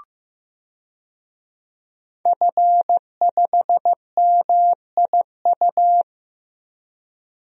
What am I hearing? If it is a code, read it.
Morse code, heard as a keyed tone, about 15 wpm, 700 Hz. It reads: F5MIU
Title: DXCC-100 Call Signs - 5 Characters - Repeated 2x - 15wpm